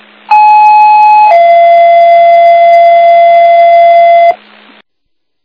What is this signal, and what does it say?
Two-tone page, heard as a one-second tone, then a three-second tone, about 850 then 680 Hz. It is heard from 0.3 to 4.3 s.
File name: Old Forge